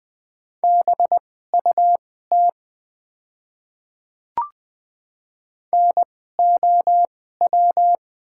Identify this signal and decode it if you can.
Morse code, heard as a keyed tone, about 20 wpm, spaced as for 14 wpm, 700 Hz. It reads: BUT NOW
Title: 2x Character Spacing - Top 200 Words - 20wpm